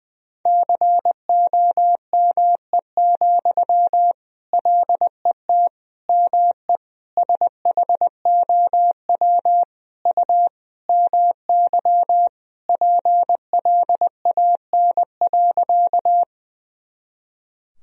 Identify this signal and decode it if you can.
Morse code, heard as a keyed tone, about 20 wpm, 700 Hz. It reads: COME, LET ME SHOW U MY PLAN.
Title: Sentences from Top 500 Words and Abbreviations 20wpm